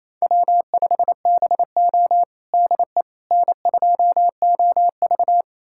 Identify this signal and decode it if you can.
Morse code, heard as a keyed tone, about 27 wpm, 700 Hz. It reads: W5BO DE N2OV
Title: USA Callsign DE Callsign - 4 Characters - 28wpm